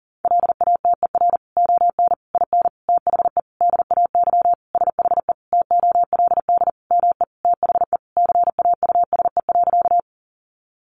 Morse code, heard as a keyed tone, about 40 wpm, 700 Hz. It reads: LATER ON IN THE DAY SHE TOLD ME THE CAUSE.
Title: Sentences from Top 700 Words 40wpm